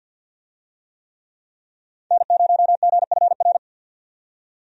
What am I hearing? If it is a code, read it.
Morse code, heard as a keyed tone, about 50 wpm, 700 Hz. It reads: N0GFR
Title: USA Call Signs - 5 Characters - 50wpm